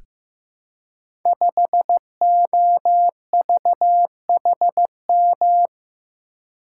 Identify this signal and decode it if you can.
Morse code, heard as a keyed tone, about 15 wpm, 700 Hz. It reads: 5OVHM
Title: Code Groups - 5 Letter:Numbers - 15wpm